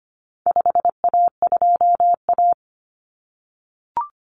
Morse code, heard as a keyed tone, about 25 wpm, 700 Hz. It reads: 5A2A
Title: Common Cut Numbers - 4-Digit 25wpm